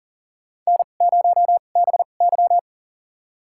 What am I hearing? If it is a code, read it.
Morse code, heard as a keyed tone, about 40 wpm, spaced as for 29 wpm, 700 Hz. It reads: N0BY
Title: USA Call Signs - 4 Characters - 2x Character Spacing - 40wpm